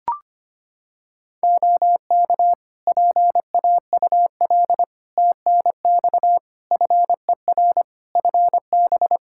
Morse code, heard as a keyed tone, about 25 wpm, 700 Hz. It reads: OK PAUL TNX FER FB
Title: K1USN SST QSO Practice 25wpm